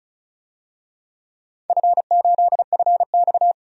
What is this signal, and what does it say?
Morse code, heard as a keyed tone, about 36 wpm, 700 Hz. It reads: F8FX
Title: DXCC-100 Call Signs - 4 Characters - 35wpm